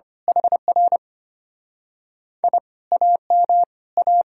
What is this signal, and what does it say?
Morse code, heard as a keyed tone, about 24 wpm, 700 Hz. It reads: HR I AM A
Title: Speed-Racing Common QSO Phrases - 25wpm